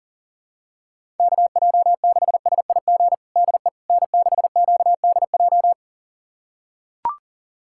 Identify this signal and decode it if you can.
Morse code, heard as a keyed tone, about 39 wpm, 700 Hz. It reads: KJ6SIG DE N6QDJ
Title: USA Callsign DE Callsign 40wpm